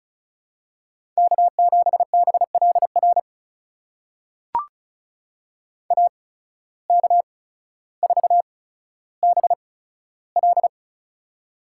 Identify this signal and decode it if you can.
Morse code, heard as a keyed tone, about 36 wpm, spaced as for 15 wpm, 700 Hz. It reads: K7BLR AK4BL
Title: USA Call Signs - 5 Characters - 8x Character Spacing - 35wpm